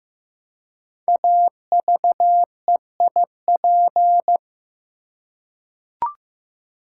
Morse code, heard as a keyed tone, about 15 wpm, 700 Hz. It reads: AVEIP
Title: Code Groups - 5 Letters - 15wpm